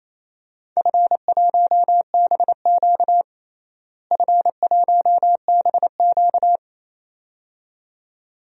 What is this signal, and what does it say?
Morse code, heard as a keyed tone, about 27 wpm, 700 Hz. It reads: F1BQ F1BQ
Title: DXCC-100 Call Signs - 4 Characters - Repeated 2x - 28wpm